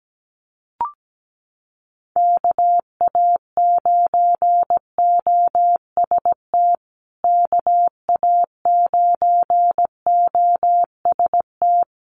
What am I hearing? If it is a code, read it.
Morse code, heard as a keyed tone, about 17 wpm, 700 Hz. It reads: KA9OST KA9OST
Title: USA Call Signs Repeated 2x 17wpm